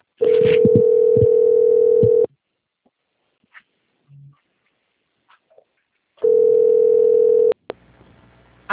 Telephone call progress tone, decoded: ringback tone

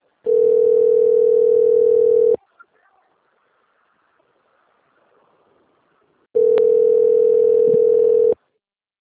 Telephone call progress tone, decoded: ringback tone

